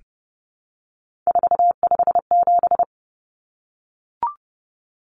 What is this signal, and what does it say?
Morse code, heard as a keyed tone, about 30 wpm, 700 Hz. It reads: 457
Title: Numbers - 3-Digit 30wpm